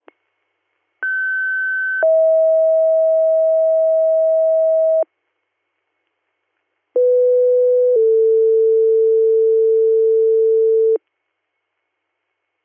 Two-tone page, heard as a one-second tone, then a three-second tone, about 1550 then 640 Hz, about 500 then 450 Hz. It is heard from 1.0 to 11.0 s.